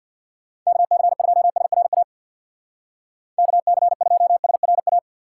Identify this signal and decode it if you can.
Morse code, heard as a keyed tone, about 62 wpm, 700 Hz. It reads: KC2SRA KC2SRA
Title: Speed Racing - USA 2x3 Call Signs 50wpm